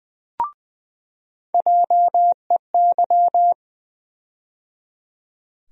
Morse code, heard as a keyed tone, about 20 wpm, 700 Hz. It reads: JEY
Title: USA Call Sign Postfixes - 3 Letters 20wpm